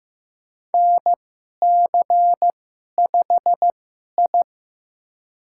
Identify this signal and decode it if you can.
Morse code, heard as a keyed tone, about 15 wpm, spaced as for 11 wpm, 700 Hz. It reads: NC5I
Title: USA Call Signs - 4 Characters - 2x Character Spacing - 15wpm